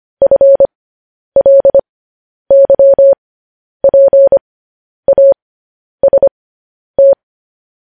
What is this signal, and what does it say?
Morse code, heard as a keyed tone, about 25 wpm, spaced as for 10 wpm, 550 Hz. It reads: FLYPAST
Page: News Headlines in Morse Code at 10 WPM